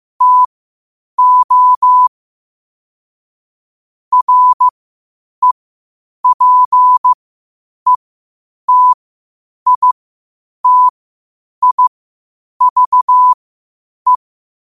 Morse code, heard as a keyed tone, about 15 wpm, spaced as for 8 wpm, 1000 Hz. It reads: TO REPETITIVE
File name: qotdfiles / 8wpm